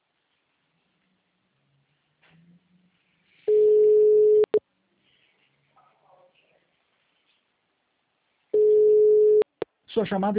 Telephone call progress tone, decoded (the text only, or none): ringback tone